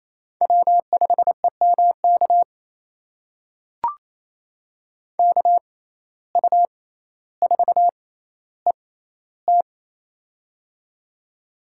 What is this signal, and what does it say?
Morse code, heard as a keyed tone, about 28 wpm, spaced as for 14 wpm, 700 Hz. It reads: W5EMK KU4ET
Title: USA Call Signs - 5 Characters - 6x Character Spacing - 28wpm